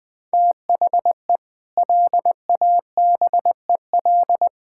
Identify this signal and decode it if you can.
Morse code, heard as a keyed tone, about 21 wpm, 700 Hz. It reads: THE LABEL
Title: Sentences with 4-Word Phrases - 20wpm